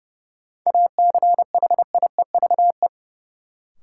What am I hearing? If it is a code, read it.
Morse code, heard as a keyed tone, about 30 wpm, 700 Hz. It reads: ACHIEVE
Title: Sentences from Top 1000 Words - Review Part 5 - New Word Drill - 30wpm